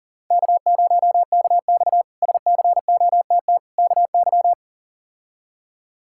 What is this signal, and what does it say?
Morse code, heard as a keyed tone, about 40 wpm, 700 Hz. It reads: K0KX SCOTT KY
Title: K1USN SST QSO Practice - 40wpm